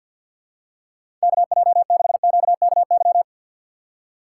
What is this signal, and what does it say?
Morse code, heard as a keyed tone, about 50 wpm, 700 Hz. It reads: KJ6QKY
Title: USA Call Signs - 6 Characters - 4x Character Spacing - 50wpm